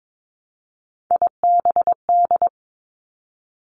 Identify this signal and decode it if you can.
Morse code, heard as a keyed tone, about 22 wpm, 700 Hz. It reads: IBD
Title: Three Challenging Letter-Number Combinations 22wpm